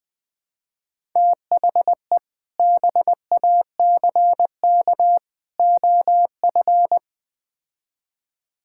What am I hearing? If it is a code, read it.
Morse code, heard as a keyed tone, about 20 wpm, 700 Hz. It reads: THE BACK OF
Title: Build A Phrase - 3-Word Phrases - 20wpm